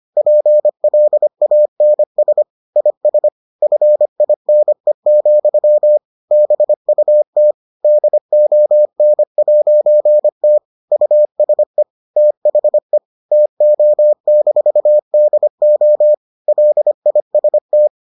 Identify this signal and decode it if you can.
Morse code, heard as a keyed tone, about 25 wpm, 600 Hz. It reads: PLANS IS FINE, BUT DON'T USE THE TO-DO LIST